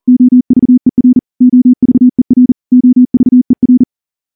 Morse code, heard as a keyed tone, about 38 wpm, 250 Hz. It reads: OVER OVER OVER
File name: morse-training-40WPM